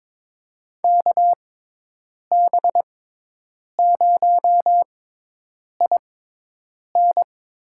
Morse code, heard as a keyed tone, about 22 wpm, spaced as for 8 wpm, 700 Hz. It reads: KB0IN
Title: USA Call Signs - 6x Character Spacing - 22wpm